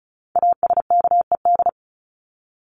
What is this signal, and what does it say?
Morse code, heard as a keyed tone, about 34 wpm, 700 Hz. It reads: ASKED
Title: ED Suffix - Top 100 Words with Base Words 35wpm